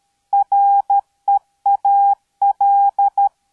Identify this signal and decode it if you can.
Morse code, heard as a keyed tone, about 13 wpm, 800 Hz. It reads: REAL